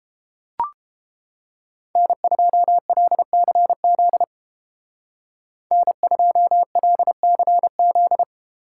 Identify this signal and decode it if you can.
Morse code, heard as a keyed tone, about 30 wpm, 700 Hz. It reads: N2LCZ N2LCZ